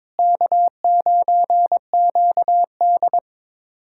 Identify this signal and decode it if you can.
Morse code, heard as a keyed tone, about 22 wpm, 700 Hz. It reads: K9QD
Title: USA Callsign DE Callsign - 4 Characters - 22wpm